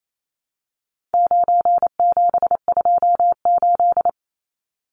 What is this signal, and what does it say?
Morse code, heard as a keyed tone, about 28 wpm, 700 Hz. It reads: 9728
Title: Numbers - 4-Digit 28wpm